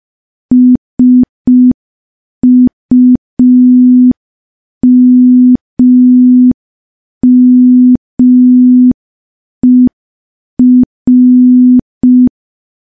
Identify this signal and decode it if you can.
Morse code, heard as a keyed tone, about 5 wpm, 250 Hz. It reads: SUMMER
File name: morse / morse-training-05WPM